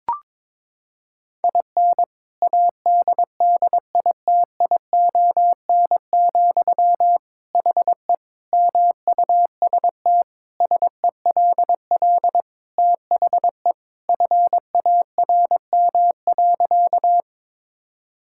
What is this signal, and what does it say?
Morse code, heard as a keyed tone, about 22 wpm, 700 Hz. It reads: IN ADDITION, HE MUST SELL THE FARM.
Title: Sentences from Top 1000 Words - Review Part 6 - 22wpm